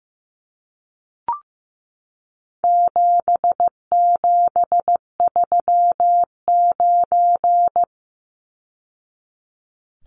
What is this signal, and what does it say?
Morse code, heard as a keyed tone, about 15 wpm, 700 Hz. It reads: 7739
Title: Numbers - 4-Digit 15wpm